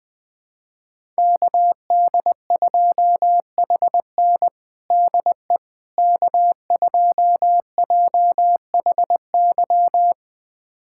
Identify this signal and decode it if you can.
Morse code, heard as a keyed tone, about 20 wpm, 700 Hz. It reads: KD2HN DE K2JHY
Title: USA Callsign DE Callsign - 5 Characters - 20wpm